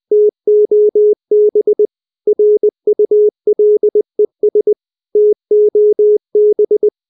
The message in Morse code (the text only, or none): TOB RULES TOB